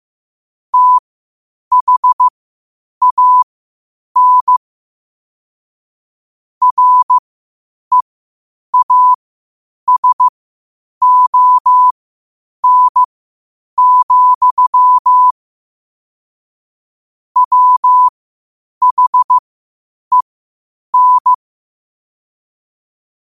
Morse code, heard as a keyed tone, about 15 wpm, spaced as for 8 wpm, 1000 Hz. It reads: THAN REASON, WHEN
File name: qotdfiles / 8wpm